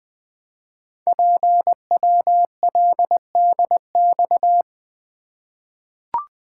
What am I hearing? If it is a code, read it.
Morse code, heard as a keyed tone, about 20 wpm, 700 Hz. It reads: PWLDX
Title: Code Groups - 5 Letter:Numbers - 20wpm